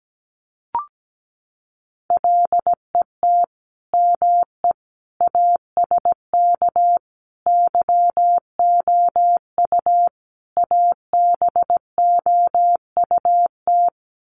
Morse code, heard as a keyed tone, about 17 wpm, 700 Hz. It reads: LET ME ASK YOU ABOUT
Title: Sets of 5 Words 17wpm